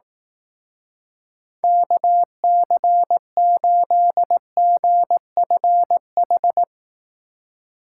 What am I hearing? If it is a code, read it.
Morse code, heard as a keyed tone, about 18 wpm, 700 Hz. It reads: KC8GFH
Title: Speed Racing - USA 2x3 Call Signs 15wpm